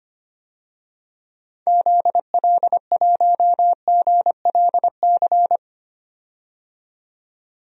Morse code, heard as a keyed tone, about 25 wpm, 700 Hz. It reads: ZL1GLC
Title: DXCC-100 Call Signs - 6 Characters - 25wpm